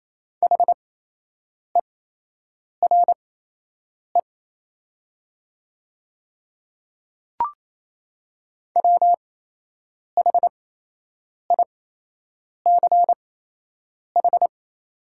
Morse code, heard as a keyed tone, about 28 wpm, spaced as for 8 wpm, 700 Hz. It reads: HERE WHICH
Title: 8x Character Spacing - Top 300 Words - 28wpm